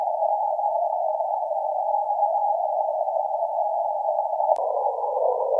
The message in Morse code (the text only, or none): none